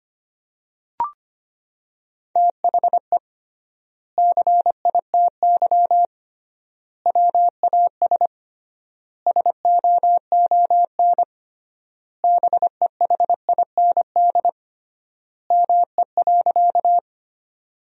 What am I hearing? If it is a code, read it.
Morse code, heard as a keyed tone, about 25 wpm, 700 Hz. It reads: THE CITY WAS SOON BEHIND ME.
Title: 3x Word-Spacing - Sentences from Top 500 Words - 25wpm